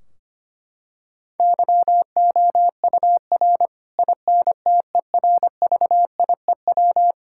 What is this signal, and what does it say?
Morse code, heard as a keyed tone, about 25 wpm, 700 Hz. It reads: YOUR INTERVIEW
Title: Sentences from Top 1600 Words 25wpm